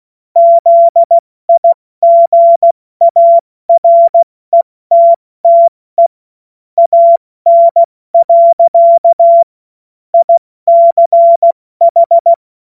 Morse code, heard as a keyed tone, about 16 wpm, 680 Hz. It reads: ZIGARETTE AN. ICH
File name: Morsekurs / Witze / Witz 27. Juli 2021, 70 F80